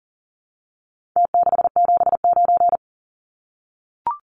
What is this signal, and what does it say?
Morse code, heard as a keyed tone, about 40 wpm, 700 Hz. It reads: T679
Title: Numbers - Contest Style - 4-Digit 40wpm